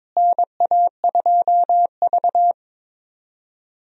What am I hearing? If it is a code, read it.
Morse code, heard as a keyed tone, about 22 wpm, 700 Hz. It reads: NA2V
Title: USA Call Signs - 4 Characters - Repeated 3x - 22wpm